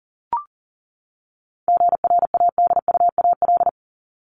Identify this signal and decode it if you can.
Morse code, heard as a keyed tone, about 40 wpm, 700 Hz. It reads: GRADUAL
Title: LY Suffix - Top 100 Words with Base Words 40wpm